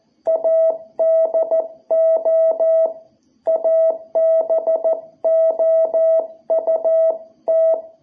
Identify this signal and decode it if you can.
Morse code, heard as a keyed tone, about 14 wpm, 660 Hz. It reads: ADO ABOUT
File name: jh8cbh / 2021cw